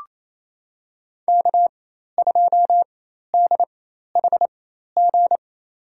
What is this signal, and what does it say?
Morse code, heard as a keyed tone, about 28 wpm, spaced as for 13 wpm, 700 Hz. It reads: K2DHG
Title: USA Call Signs - 5 Characters - 4x Character Spacing - 28wpm